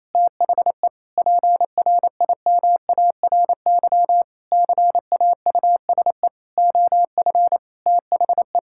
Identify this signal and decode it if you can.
Morse code, heard as a keyed tone, about 28 wpm, 700 Hz. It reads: THE PRIMARY CAUSE OF THE
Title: Sentences from Top 1000 Words - Review Part 5 - 28wpm